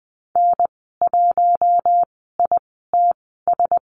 Morse code, heard as a keyed tone, about 20 wpm, spaced as for 15 wpm, 700 Hz. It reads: N1ITS
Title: 055 Lesson 10g - Letter r - Call Signs 20wpm